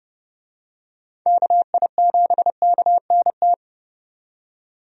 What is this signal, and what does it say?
Morse code, heard as a keyed tone, about 32 wpm, 700 Hz. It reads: KI7KNT
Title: USA Call Signs - 6 Characters - 30wpm